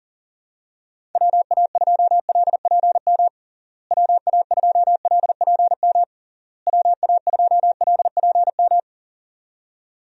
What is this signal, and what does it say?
Morse code, heard as a keyed tone, about 40 wpm, 700 Hz. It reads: WA2LPM WA2LPM WA2LPM
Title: USA Call Signs - 6 Characters - Repeated 3x - 40wpm